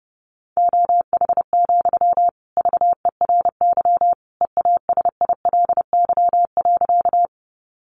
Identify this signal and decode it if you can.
Morse code, heard as a keyed tone, about 28 wpm, 700 Hz. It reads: OH, VERY EASILY.
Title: Sentences from Top 900 Words 30wpm